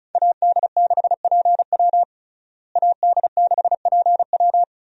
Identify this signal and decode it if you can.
Morse code, heard as a keyed tone, about 36 wpm, 700 Hz. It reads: AD6PW AD6PW
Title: USA Call Signs - 5 Characters - Repeated 2x - 35wpm